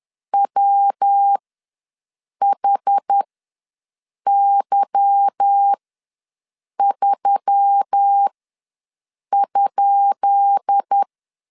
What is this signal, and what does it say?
Morse code, heard as a keyed tone, about 10 wpm, spaced as for 6 wpm, 800 Hz. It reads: WHY3?
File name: LEZIONI / 10 SIGNS / 10B INT ws 10wpm 4134